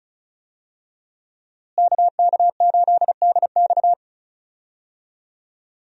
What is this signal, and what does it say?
Morse code, heard as a keyed tone, about 34 wpm, 700 Hz. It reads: KK8DX
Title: USA Call Signs - 5 Characters - Repeated 2x - 35wpm